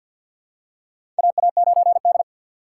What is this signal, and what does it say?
Morse code, heard as a keyed tone, about 51 wpm, 700 Hz. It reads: AA9D
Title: USA Call Signs - 6x Character Spacing - 50wpm